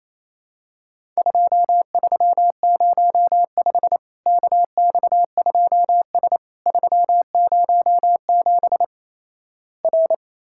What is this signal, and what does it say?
Morse code, heard as a keyed tone, about 28 wpm, 700 Hz. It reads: 2305 KX2S 307 R
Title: Worked All Europe DX Contest - QTC Exchange - 28wpm